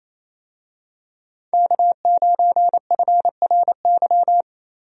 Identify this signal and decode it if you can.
Morse code, heard as a keyed tone, about 27 wpm, 700 Hz. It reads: K9FRY
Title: USA Call Signs - 5 Characters - Repeated 3x - 28wpm